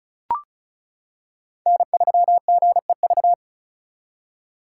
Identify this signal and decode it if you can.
Morse code, heard as a keyed tone, about 36 wpm, 700 Hz. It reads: N3GEV
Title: Code Groups - 5 Letter:Numbers - 35wpm